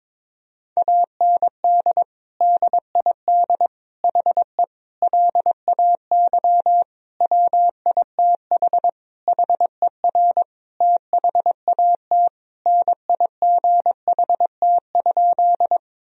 Morse code, heard as a keyed tone, about 22 wpm, 700 Hz. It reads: AND DID HE LAY WITH HER THAT NIGHT?
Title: Sentences from Top 1000 Words - Review Part 2 - 22wpm